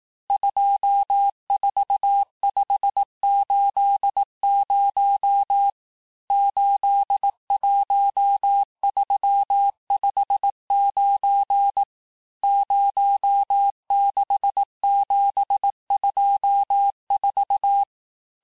Morse code, heard as a keyed tone, about 18 wpm, 800 Hz. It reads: 24580 81359 06724
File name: cours-du-lundi files / cours-1852-texte-chiffre-vitesse-17MPM